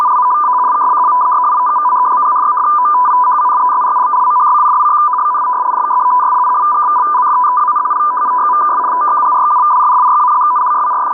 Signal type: two-tone teleprinter signal